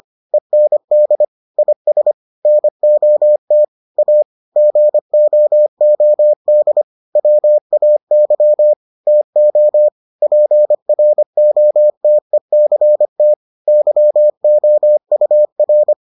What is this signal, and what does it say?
Morse code, heard as a keyed tone, about 25 wpm, 600 Hz. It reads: END IS NOT A GOOD WAY TO PROTECT YOUR